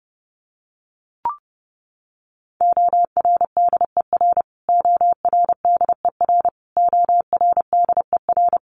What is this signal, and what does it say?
Morse code, heard as a keyed tone, about 30 wpm, 700 Hz. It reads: ORDER ORDER ORDER
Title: Top 500 Words Repeated 30wpm